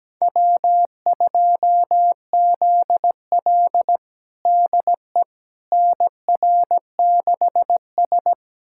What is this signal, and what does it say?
Morse code, heard as a keyed tone, about 17 wpm, 700 Hz. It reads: W2ZL DE NR6S
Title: USA Callsign DE Callsign - 4 Characters - 17wpm